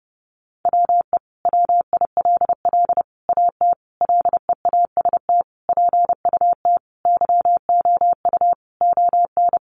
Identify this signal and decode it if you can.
Morse code, heard as a keyed tone, about 30 wpm, 700 Hz. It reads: WE WILL AT LEAST PUT YOU ON
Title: Sentences from Top 1200 Words 30wpm